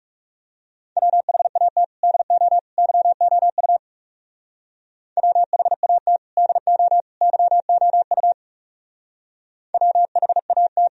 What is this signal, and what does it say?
Morse code, heard as a keyed tone, about 35 wpm, 700 Hz. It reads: WHAT DO YOU WHAT DO YOU WHAT
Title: Speed-Racing - Sets of 3 Words 30wpm